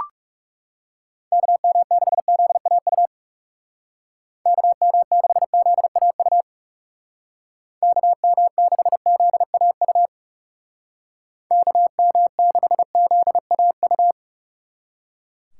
Morse code, heard as a keyed tone, about 40 wpm, 700 Hz. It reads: KM6ZAU KM6ZAU KM6ZAU KM6ZAU